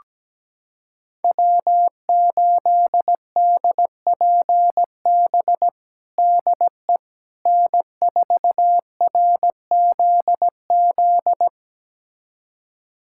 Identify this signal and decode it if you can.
Morse code, heard as a keyed tone, about 17 wpm, 700 Hz. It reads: W8DPB DE N4RZZ
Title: USA Callsign DE Callsign - 5 Characters - 17wpm